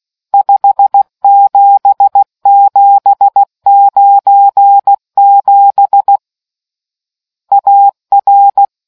Morse code, heard as a keyed tone, about 16 wpm, 800 Hz. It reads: 57797 AR